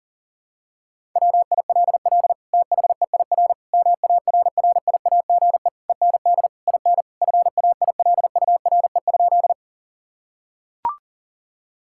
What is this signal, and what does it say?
Morse code, heard as a keyed tone, about 40 wpm, 700 Hz. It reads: WILL THEIR MARRIAGE END IN FAILURE?
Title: Sentences from Top 1000 Words - Review Part 6 - 40wpm